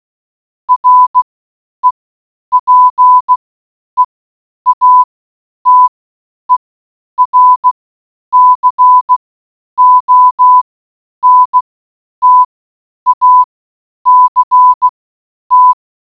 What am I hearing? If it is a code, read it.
Morse code, heard as a keyed tone, about 16 wpm, spaced as for 9 wpm, 1000 Hz. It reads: REPEATER CONTACT